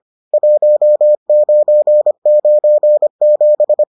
Morse code, heard as a keyed tone, about 25 wpm, 600 Hz. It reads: 1997